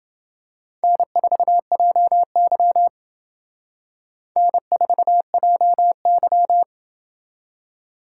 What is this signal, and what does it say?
Morse code, heard as a keyed tone, about 28 wpm, 700 Hz. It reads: N4JY N4JY